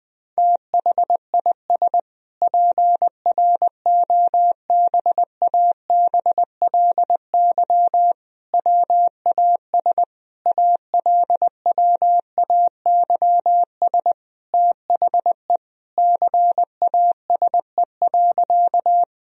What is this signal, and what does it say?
Morse code, heard as a keyed tone, about 21 wpm, 700 Hz. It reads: THIS PROBABLY WAS ALWAYS THE CASE.
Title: Sentences from Top 1000 Words - Review Part 4 - 20wpm